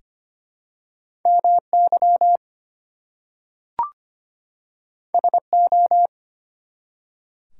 Morse code, heard as a keyed tone, about 25 wpm, 700 Hz. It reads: MY SO
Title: Two Letter Words - 25wpm